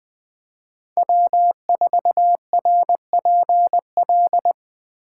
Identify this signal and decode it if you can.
Morse code, heard as a keyed tone, about 20 wpm, 700 Hz. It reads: W4RPL